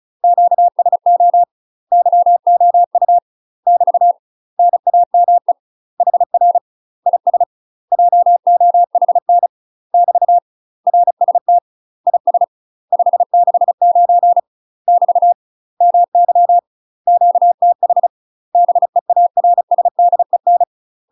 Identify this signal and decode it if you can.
Morse code, heard as a keyed tone, about 36 wpm, 700 Hz. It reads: QSO YOU <BT> NAME HR IS JOHN <BT> RST IS 569 <BT> MY QTH BEARSDEN
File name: UK MASTER QSOs TRAINING / 35WPM / UK MASTER QSOs 0032